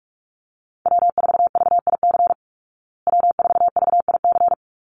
Morse code, heard as a keyed tone, about 44 wpm, 700 Hz. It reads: W4VIC W4VIC
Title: Call Signs Repeated 2x - 45wpm